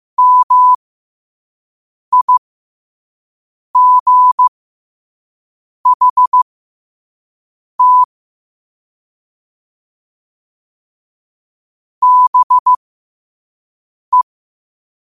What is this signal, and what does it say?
Morse code, heard as a keyed tone, about 15 wpm, spaced as for 5 wpm, 1000 Hz. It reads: MIGHT BE
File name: qotdfiles / 5wpm